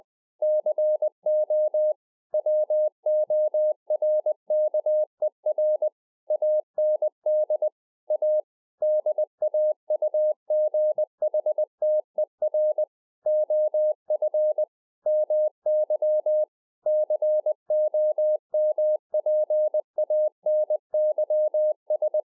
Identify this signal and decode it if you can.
Morse code, heard as a keyed tone, about 20 wpm, 610 Hz. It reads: CO WORKER AND A DAUGHTER OF MY COMPANYS